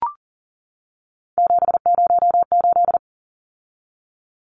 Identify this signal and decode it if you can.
Morse code, heard as a keyed tone, about 40 wpm, 700 Hz. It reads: 708
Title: Numbers - 3-Digit 40wpm